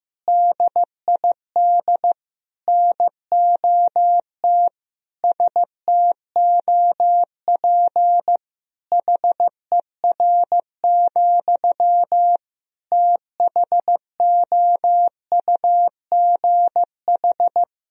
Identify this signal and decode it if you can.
Morse code, heard as a keyed tone, about 16 wpm, 700 Hz. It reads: DID NOT STOP HER, THOUGH
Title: Sentences from Top 1000 Words - Review Part 2 - 15wpm